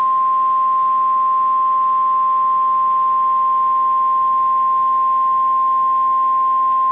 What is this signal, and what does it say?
Emergency Alert System (SAME) tones (the 1050 Hz attention tone). It is heard from 0.8 to 6.5 s.